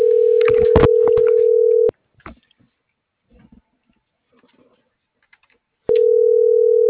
Ringback tone (North America), 440 + 480 Hz.